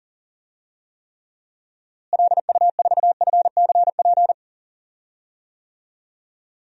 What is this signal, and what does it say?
Morse code, heard as a keyed tone, about 40 wpm, 700 Hz. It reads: LU4FCP
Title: DXCC-100 Call Signs - 6 Characters - Repeated 2x - 40wpm